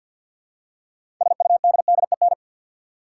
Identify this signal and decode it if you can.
Morse code, heard as a keyed tone, about 47 wpm, 700 Hz. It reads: SUDDEN
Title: Six Letter Words - 50wpm